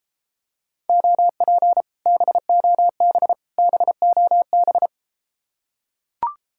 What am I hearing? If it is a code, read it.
Morse code, heard as a keyed tone, about 33 wpm, 700 Hz. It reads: OP BOB BOB